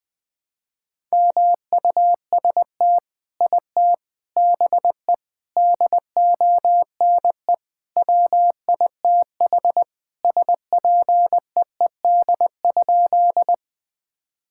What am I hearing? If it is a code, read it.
Morse code, heard as a keyed tone, about 20 wpm, 700 Hz. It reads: MUST IT BE DONE WITH SPEED?
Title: Sentences from Top 1000 Words - Review Part 5 - 20wpm